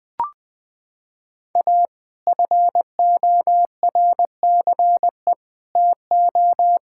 Morse code, heard as a keyed tone, about 20 wpm, 700 Hz. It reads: A FORCE TO
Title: Sets of 6 Words - 20wpm